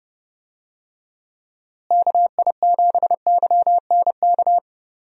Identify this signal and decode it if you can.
Morse code, heard as a keyed tone, about 30 wpm, 700 Hz. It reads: KI7YNK
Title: USA Call Signs - 6 Characters - Repeated 3x - 30wpm